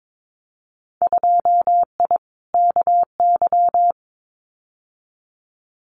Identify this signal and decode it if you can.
Morse code, heard as a keyed tone, about 22 wpm, 700 Hz. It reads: 2I KY
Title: Winter Field Day Exchanges 22wpm